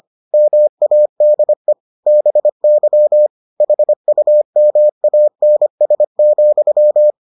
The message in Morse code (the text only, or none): MADE BY HUMANS,